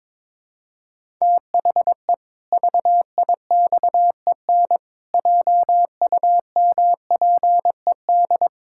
Morse code, heard as a keyed tone, about 22 wpm, 700 Hz. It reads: THE VIXEN JUMPED
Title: Pangram Challenge v2 - 22wpm